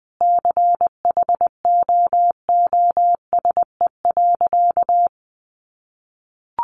Morse code, heard as a keyed tone, about 20 wpm, 700 Hz. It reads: CHOOSE.
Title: Sentences from Top 700 Words 20wpm